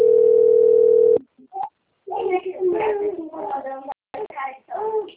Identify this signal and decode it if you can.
Ringback tone (North America), 440 + 480 Hz, with speech.
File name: out- - - - - .7421